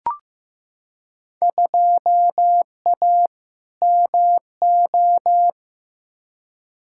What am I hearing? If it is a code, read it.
Morse code, heard as a keyed tone, about 15 wpm, 700 Hz. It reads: 2A MO
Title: Field Day Exchanges Encore - 15wpm